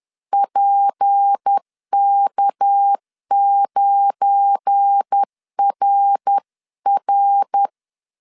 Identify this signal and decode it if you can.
Morse code, heard as a keyed tone, about 10 wpm, 800 Hz. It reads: PK9RR
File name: CW / DET / 1 9 1500C S A 10wpm 4511